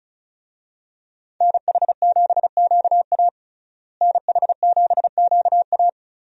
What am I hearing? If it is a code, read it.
Morse code, heard as a keyed tone, about 36 wpm, 700 Hz. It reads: NH7QA NH7QA
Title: USA Call Signs - 5 Characters - Repeated 2x - 35wpm